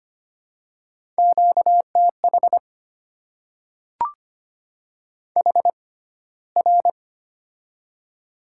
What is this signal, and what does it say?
Morse code, heard as a keyed tone, about 25 wpm, spaced as for 15 wpm, 690 Hz. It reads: QTH HR
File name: Top 90 QSO Elements - 6x Character Spacing - 25wpm